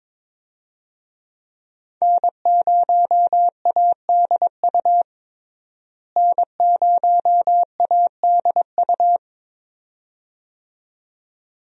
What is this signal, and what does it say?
Morse code, heard as a keyed tone, about 22 wpm, 700 Hz. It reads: N0ADU N0ADU
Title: USA Call Signs - 5 Characters - Repeated 2x - 22wpm